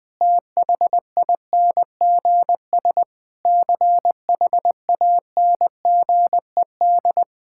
Morse code, heard as a keyed tone, about 20 wpm, 700 Hz. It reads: THINGS CHANGED
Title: Sentences with 5-Word Phrases - 20wpm